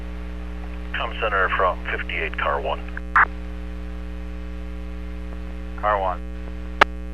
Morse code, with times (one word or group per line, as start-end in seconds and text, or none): none